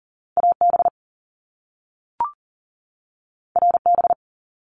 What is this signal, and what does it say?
Morse code, heard as a keyed tone, about 40 wpm, 700 Hz. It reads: AB RB
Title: Top Two Letter Combinations in Words 40wpm